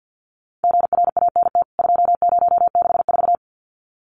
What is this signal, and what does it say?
Morse code, heard as a keyed tone, about 50 wpm, 700 Hz. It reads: GRANT 2064
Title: CWT Exchange 50wpm